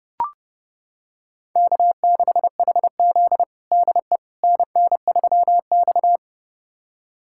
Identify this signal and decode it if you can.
Morse code, heard as a keyed tone, about 30 wpm, 700 Hz. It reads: K6HZ DE NN3X
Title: USA Callsign DE Callsign - 4 Characters - 30wpm